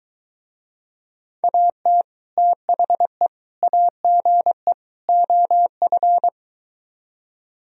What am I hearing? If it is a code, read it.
Morse code, heard as a keyed tone, about 23 wpm, 700 Hz. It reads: AT THE AGE OF